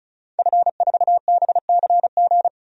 Morse code, heard as a keyed tone, about 36 wpm, 700 Hz. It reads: F4BCG
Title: DXCC-100 Call Signs - 5 Characters - 35wpm